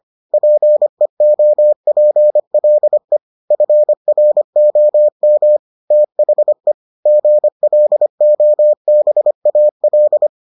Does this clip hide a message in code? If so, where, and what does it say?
Yes: Morse code from 0.3 to 10.3 s, heard as a keyed tone, about 25 wpm, 600 Hz. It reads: PEOPLE FROM THE GLOBAL